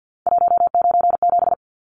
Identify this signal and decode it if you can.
Morse code, heard as a keyed tone, about 50 wpm, 700 Hz. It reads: 197
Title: Numbers - 3-Digit 50wpm